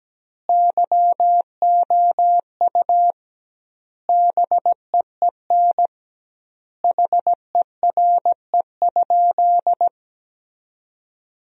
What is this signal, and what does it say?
Morse code, heard as a keyed tone, about 17 wpm, 700 Hz. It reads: YOU BEEN HERE?
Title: 2x Word-Spacing - Sentences from Top 200 Words - 17wpm